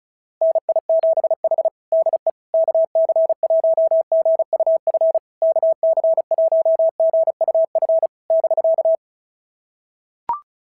Morse code, heard as a keyed tone, about 33 wpm, 650 Hz. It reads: NI7H DE KC1GUF KC1GUF <BK>
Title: SOTA QSO Practice 35wpm